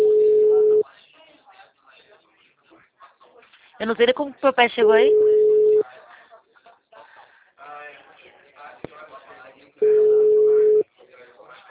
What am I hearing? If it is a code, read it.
Ringback tone (Europe, 425 Hz), with speech.
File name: out- -2040- - - .8439